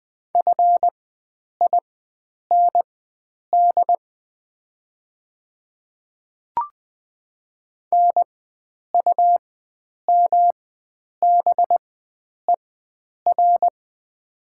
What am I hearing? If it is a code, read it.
Morse code, heard as a keyed tone, about 20 wpm, spaced as for 9 wpm, 700 Hz. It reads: FIND NUMBER
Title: 4x Character Spacing - Top 100 Words - 20wpm